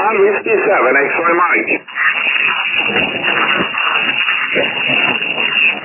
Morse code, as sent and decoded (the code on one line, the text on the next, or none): none